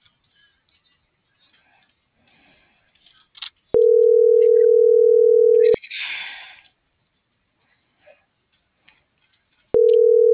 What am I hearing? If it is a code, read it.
Ringback tone (North America), 440 + 480 Hz.